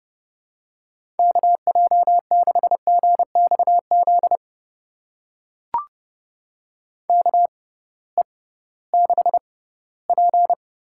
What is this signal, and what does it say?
Morse code, heard as a keyed tone, about 30 wpm, spaced as for 18 wpm, 700 Hz. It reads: KJ6GXZ KE6P
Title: USA Call Signs - 6 Characters - 6x Character Spacing - 30wpm